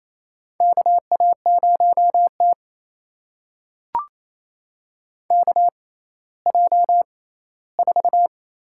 Morse code, heard as a keyed tone, about 28 wpm, spaced as for 16 wpm, 700 Hz. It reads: KA0T KJ4
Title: USA Call Signs - 6x Character Spacing - 28wpm